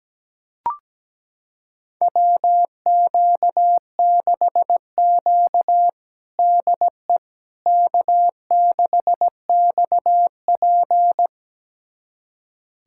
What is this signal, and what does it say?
Morse code, heard as a keyed tone, about 17 wpm, 700 Hz. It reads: WQ6Q DE K6XP
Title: USA Callsign DE Callsign - 4 Characters - 17wpm